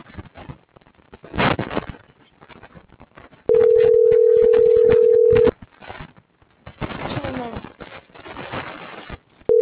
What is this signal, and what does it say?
Ringback tone (North America), 440 + 480 Hz.